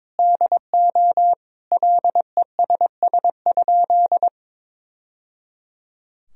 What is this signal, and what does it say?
Morse code, heard as a keyed tone, about 22 wpm, 700 Hz. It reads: DO LESS?
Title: Sentences from Top 1000 Words - Review Part 2 - 22wpm